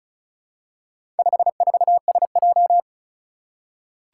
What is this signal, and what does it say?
Morse code, heard as a keyed tone, about 36 wpm, 700 Hz. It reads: 54SJ
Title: Four Challenging Letter-Number Combinations 35wpm